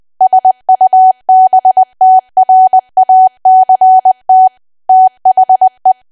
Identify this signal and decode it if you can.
Morse code, heard as a keyed tone, about 20 wpm, 750 Hz. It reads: SUBTRACT THE
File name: cwa / W1AW-Level3 / 20WPM